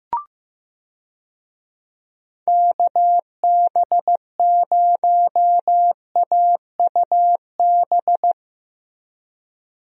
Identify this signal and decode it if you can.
Morse code, heard as a keyed tone, about 15 wpm, 700 Hz. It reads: KB0AUB
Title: USA Call Signs - 6 Characters - Repeated 3x - 15wpm